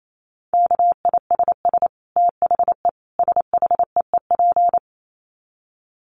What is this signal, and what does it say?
Morse code, heard as a keyed tone, about 28 wpm, 700 Hz. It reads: KISS THE SHEEP
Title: Dit Challenge 28wpm